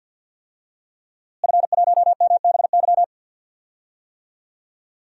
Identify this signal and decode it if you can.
Morse code, heard as a keyed tone, about 50 wpm, 700 Hz. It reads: F1MBY
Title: DXCC-100 Call Signs - 5 Characters - Repeated 3x - 50wpm